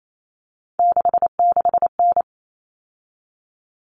Morse code, heard as a keyed tone, about 28 wpm, 700 Hz. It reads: 66N
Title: All Cut Numbers - 3-Digit 28wpm